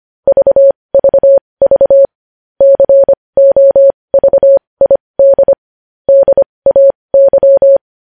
Morse code, heard as a keyed tone, about 25 wpm, spaced as for 20 wpm, 550 Hz. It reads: VVV COVID DAY
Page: news-headlines-in-morse-code-at-20-wpm